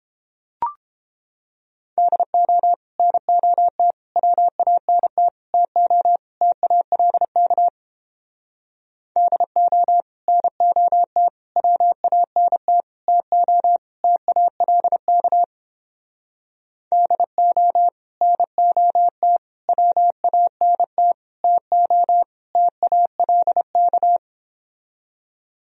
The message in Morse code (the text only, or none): DO NOT WANT TO TALK DO NOT WANT TO TALK DO NOT WANT TO TALK